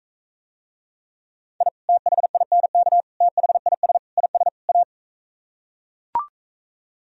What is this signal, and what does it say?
Morse code, heard as a keyed tone, about 44 wpm, 700 Hz. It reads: I THINK THIS IS A